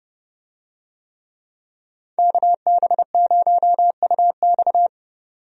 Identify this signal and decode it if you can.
Morse code, heard as a keyed tone, about 31 wpm, 700 Hz. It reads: KB0UX
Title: USA Call Signs - 5 Characters - 30wpm